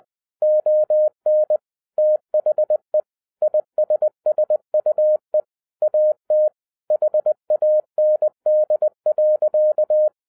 Morse code, heard as a keyed tone, about 20 wpm, 610 Hz. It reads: ON THE ISSUE AT HAND.
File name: 2011-10-mp3 / n13-TIPS FROM MY 34 YEARS OF EXPERIENCE 20wpm